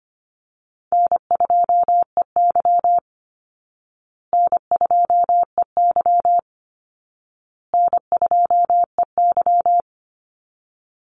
Morse code, heard as a keyed tone, about 25 wpm, 700 Hz. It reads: N2EY N2EY N2EY
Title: Call Signs Repeated 2x - 25wpm